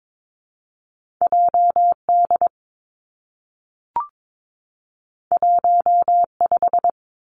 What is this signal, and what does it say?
Morse code, heard as a keyed tone, about 22 wpm, 700 Hz. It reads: J D 1 5
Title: Two Challenging Letter-Number Combinations 22wpm